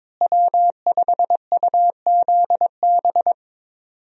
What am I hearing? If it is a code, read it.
Morse code, heard as a keyed tone, about 22 wpm, 700 Hz. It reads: W5UZB
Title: USA Call Signs - 5 Characters - Repeated 3x - 22wpm